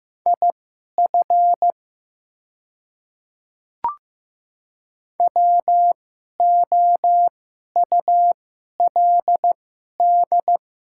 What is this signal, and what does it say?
Morse code, heard as a keyed tone, about 15 wpm, spaced as for 11 wpm, 700 Hz. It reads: IF WOULD
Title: 2x Character Spacing - Top 200 Words - 15wpm